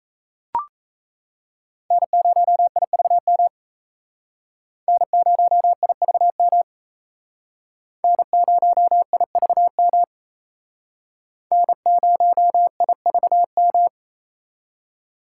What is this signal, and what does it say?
Morse code, heard as a keyed tone, about 36 wpm, 700 Hz. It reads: N0IVM N0IVM N0IVM N0IVM